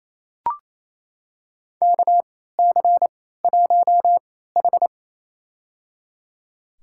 Morse code, heard as a keyed tone, about 28 wpm, spaced as for 16 wpm, 700 Hz. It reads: KC1H
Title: USA Call Signs - 4 Characters - 3x Character Spacing - 28wpm